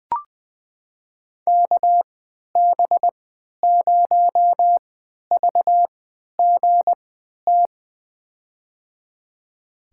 Morse code, heard as a keyed tone, about 20 wpm, spaced as for 11 wpm, 700 Hz. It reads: KB0VGT
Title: USA Call Signs - 3x Character Spacing - 20wpm